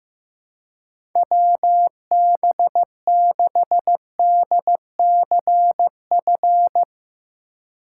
Morse code, heard as a keyed tone, about 15 wpm, 700 Hz. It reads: WB6DCF
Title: USA Call Signs - 2x Character Spacing - 15wpm